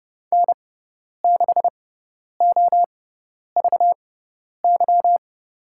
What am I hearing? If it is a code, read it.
Morse code, heard as a keyed tone, about 31 wpm, spaced as for 10 wpm, 700 Hz. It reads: N6OVY
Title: USA Call Signs - 5 Characters - 6x Character Spacing - 30wpm